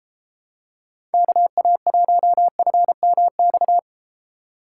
Morse code, heard as a keyed tone, about 31 wpm, 700 Hz. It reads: KA1FMX